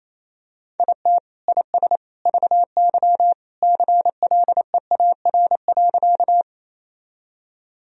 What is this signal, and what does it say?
Morse code, heard as a keyed tone, about 28 wpm, 700 Hz. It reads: IT IS VY CLEAR.
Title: Sentences from Top 500 Words and Abbreviations 28wpm